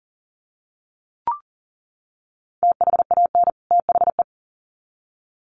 Morse code, heard as a keyed tone, about 40 wpm, 700 Hz. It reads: THAN THE
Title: Sets of 2 Words 40wpm